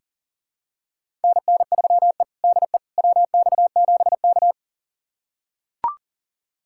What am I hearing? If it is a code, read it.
Morse code, heard as a keyed tone, about 37 wpm, 700 Hz. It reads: NN3E DE WX7K
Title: USA Callsign DE Callsign - 4 Characters - 40wpm